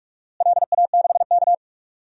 Morse code, heard as a keyed tone, about 41 wpm, 700 Hz. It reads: LA6K
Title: DXCC-100 Call Signs - 4 Characters - Repeated 2x - 45wpm